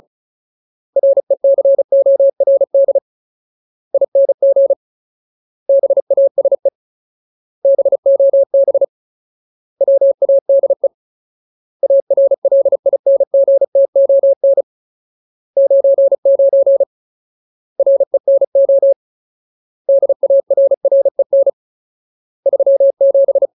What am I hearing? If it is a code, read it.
Morse code, heard as a keyed tone, about 34 wpm, 550 Hz. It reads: RECORDING BASE BOB WADE ARLINGTON 99 RENO DARREN 37
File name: Ultimate Word List / 35WPM Ultimate